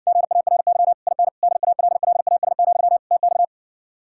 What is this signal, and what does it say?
Morse code, heard as a keyed tone, about 60 wpm, 700 Hz. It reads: GARY IN DALLAS, TX